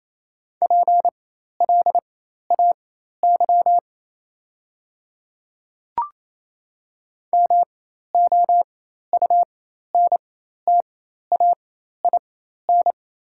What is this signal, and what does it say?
Morse code, heard as a keyed tone, about 27 wpm, spaced as for 13 wpm, 700 Hz. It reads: PLAY MOUNTAIN
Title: 4x Character Spacing - Top 300 Words - 28wpm